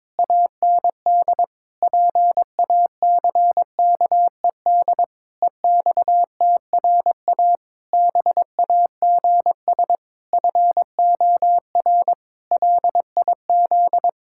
Morse code, heard as a keyed tone, about 21 wpm, 700 Hz. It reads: AND PACKED EXTRA BAGS FOR LIZ
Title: Pangram Challenge v2 - 22wpm